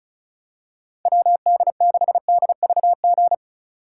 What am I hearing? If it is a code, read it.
Morse code, heard as a keyed tone, about 36 wpm, 700 Hz. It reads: WD6DVG
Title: USA Call Signs - 6 Characters - 8x Character Spacing - 35wpm